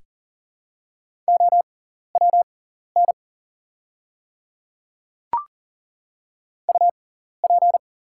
Morse code, heard as a keyed tone, about 39 wpm, spaced as for 14 wpm, 700 Hz. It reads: OWN UP
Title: 6x Character Spacing - Top 300 Words - 40wpm